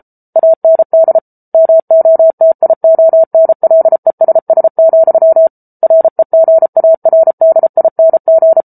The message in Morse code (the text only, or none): AND MOTIONLESS, REGARDING